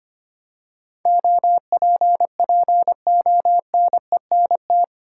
Morse code, heard as a keyed tone, about 25 wpm, 700 Hz. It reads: OPPONENT
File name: 8 Letter Words - First 3 Letters Given - 25wpm